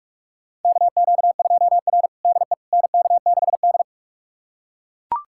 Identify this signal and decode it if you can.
Morse code, heard as a keyed tone, about 41 wpm, 700 Hz. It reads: KQ2R DE NK6D
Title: USA Callsign DE Callsign - 4 Characters - 45wpm